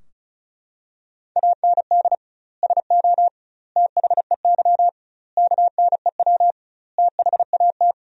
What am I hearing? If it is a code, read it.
Morse code, heard as a keyed tone, about 36 wpm, 700 Hz. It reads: AND SO THEY KNEW THAT
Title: 2x Word-Spacing - Sentences from Top 400 Words - 35wpm